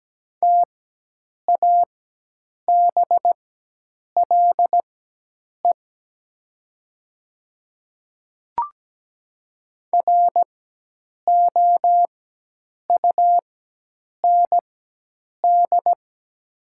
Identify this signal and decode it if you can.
Morse code, heard as a keyed tone, about 17 wpm, spaced as for 8 wpm, 700 Hz. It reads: TABLE ROUND
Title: 4x Character Spacing - Top 500 Words - 17wpm